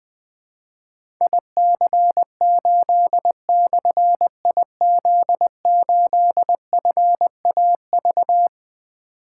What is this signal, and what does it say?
Morse code, heard as a keyed tone, about 20 wpm, 700 Hz. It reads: IC8/IZ8FAV
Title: Difficult Call Signs 20wpm